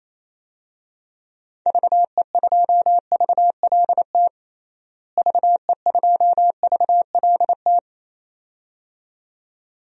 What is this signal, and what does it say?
Morse code, heard as a keyed tone, about 28 wpm, 700 Hz. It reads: VE2VLT VE2VLT
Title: DXCC-100 Call Signs - 6 Characters - Repeated 2x - 28wpm